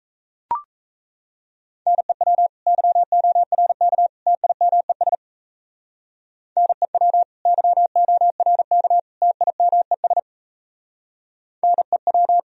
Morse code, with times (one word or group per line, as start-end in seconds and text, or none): 1.9-2.5 NEW
2.7-4.1 YORK
4.3-5.2 TIMES
6.6-7.2 NEW
7.5-9.0 YORK
9.2-10.2 TIMES
11.6-12.4 NEW